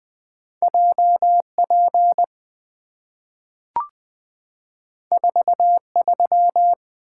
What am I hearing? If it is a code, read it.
Morse code, heard as a keyed tone, about 20 wpm, 700 Hz. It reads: JP 43
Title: Two Letter-Number Combinations - 20wpm